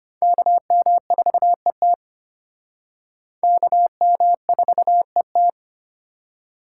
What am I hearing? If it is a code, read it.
Morse code, heard as a keyed tone, about 34 wpm, 700 Hz. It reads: KM4ET KM4ET